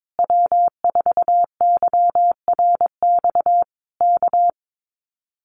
Morse code, heard as a keyed tone, about 22 wpm, 700 Hz. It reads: W4YRX K
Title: Common QSO Phrases Encore 22wpm